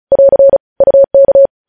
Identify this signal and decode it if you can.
Morse code, heard as a keyed tone, about 36 wpm, 550 Hz. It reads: <AR> UK
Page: news headlines in morse code at 35 wpm